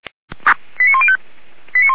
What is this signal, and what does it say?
Five-tone selective call (ZVEI), heard as quick runs of five tones, tones of about 70 ms. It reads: 89105